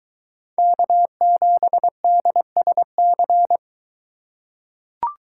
Morse code, heard as a keyed tone, about 23 wpm, 700 Hz. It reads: K7DSC